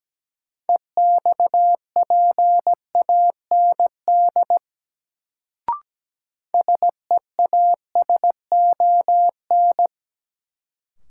Morse code, heard as a keyed tone, about 17 wpm, 700 Hz. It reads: EXPAND SEASON
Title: Sentences from Top 1400 Words - New Word Drill Alt - 17wpm